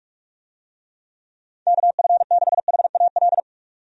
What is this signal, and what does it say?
Morse code, heard as a keyed tone, about 43 wpm, 700 Hz. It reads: KF6HAL
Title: USA Call Signs - 6 Characters - 45wpm